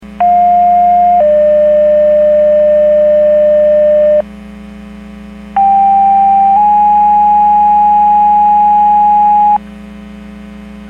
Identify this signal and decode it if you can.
Two-tone page, heard as a one-second tone, then a three-second tone, about 710 then 600 Hz, about 790 then 830 Hz. It is heard from 0.2 to 9.6 s.
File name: Portage 154.385 Paging